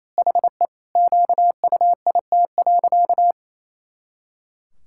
Morse code, heard as a keyed tone, about 28 wpm, 700 Hz. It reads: HE QUIT.
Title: Pangram Challenge v2 - 28wpm